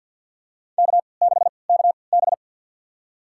Morse code, heard as a keyed tone, about 50 wpm, spaced as for 28 wpm, 700 Hz. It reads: K6KB
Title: USA Call Signs - 4 Characters - 3x Character Spacing - 50wpm